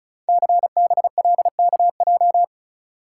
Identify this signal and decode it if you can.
Morse code, heard as a keyed tone, about 36 wpm, 700 Hz. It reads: CBLKJ
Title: Code Groups - 5 Letters - 35wpm